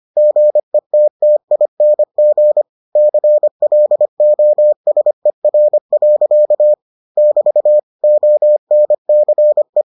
Morse code, heard as a keyed tone, about 25 wpm, 600 Hz. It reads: GETTING CLOSER. <BT> ONCE